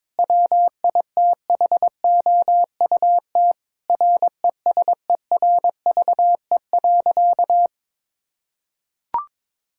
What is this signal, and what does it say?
Morse code, heard as a keyed tone, about 22 wpm, 700 Hz. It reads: WITHOUT RESERVE.
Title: Sentences from Top 1000 Words - Review Part 6 - 22wpm